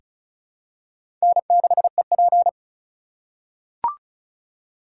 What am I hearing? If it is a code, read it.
Morse code, heard as a keyed tone, about 34 wpm, 700 Hz. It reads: N6EP
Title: USA Call Signs - 4 Characters - 3x Character Spacing - 35wpm